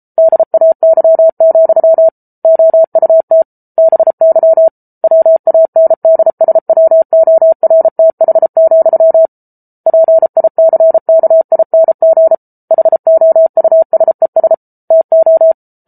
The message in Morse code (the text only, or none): DAY, OUT BY WANDSWORTH, PICKING HOUSES TO